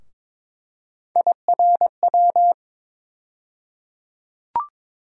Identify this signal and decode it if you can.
Morse code, heard as a keyed tone, about 22 wpm, 700 Hz. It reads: IRW
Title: USA Call Sign Postfixes - 3 Letters 22wpm